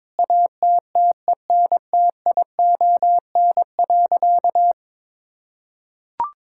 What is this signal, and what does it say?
Morse code, heard as a keyed tone, about 22 wpm, 700 Hz. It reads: ATTENTION.
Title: Sentences from Top 1000 Words - Review Part 6 - 22wpm